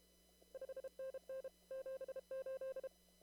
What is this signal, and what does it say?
Morse code, heard as a keyed tone, about 31 wpm, 530 Hz. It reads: 5NN 78
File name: CW / kt5j wpx cw-op-k5pi